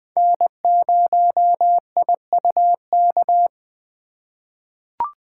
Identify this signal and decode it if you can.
Morse code, heard as a keyed tone, about 20 wpm, 700 Hz. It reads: N0IUK
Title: USA Call Signs - 5 Characters - 4x Character Spacing - 20wpm